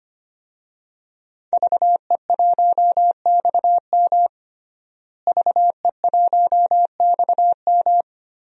Morse code, heard as a keyed tone, about 25 wpm, 700 Hz. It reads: VE1XM VE1XM
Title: DXCC-100 Call Signs - 5 Characters - Repeated 2x - 25wpm